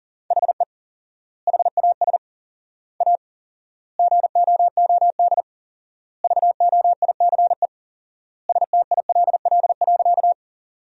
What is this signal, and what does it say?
Morse code, heard as a keyed tone, about 40 wpm, 700 Hz. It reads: HE HAS A GOOD VOICE STILL.
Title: 4x Word-Spacing - Sentences from Top 500 Words - 40wpm